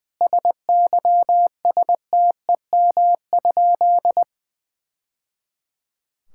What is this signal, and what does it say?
Morse code, heard as a keyed tone, about 21 wpm, 700 Hz. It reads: SYSTEM?
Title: Sentences from Top 1000 Words - Review Part 6 - 20wpm